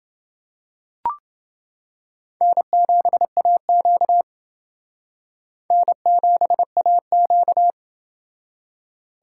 Morse code, heard as a keyed tone, about 28 wpm, 700 Hz. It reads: N7AQ N7AQ